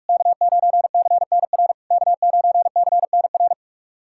Morse code, heard as a keyed tone, about 46 wpm, 690 Hz. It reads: K9CNR K9CNR
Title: SOTA QSO Practice 45wpm